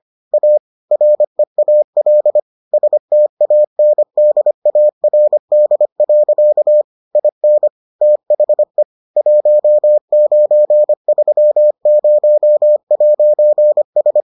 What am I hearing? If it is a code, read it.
Morse code, heard as a keyed tone, about 25 wpm, 600 Hz. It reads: A REAL STANDARD. IN THE 1930'S